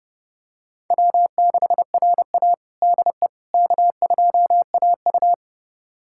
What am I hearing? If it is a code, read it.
Morse code, heard as a keyed tone, about 30 wpm, 700 Hz. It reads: W6RA DE K2AU
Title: USA Callsign DE Callsign - 4 Characters - 30wpm